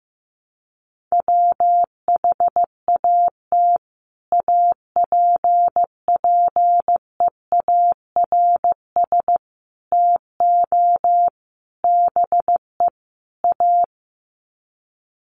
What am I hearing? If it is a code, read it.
Morse code, heard as a keyed tone, about 15 wpm, 700 Hz. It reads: WHAT APPEARS TO BE A
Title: Sets of 5 Words Encore2 15wpm